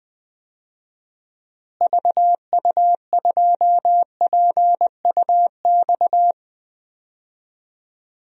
Morse code, heard as a keyed tone, about 20 wpm, 700 Hz. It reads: VU2PUX
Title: DXCC-100 Call Signs - 6 Characters - Repeated 3x - 20wpm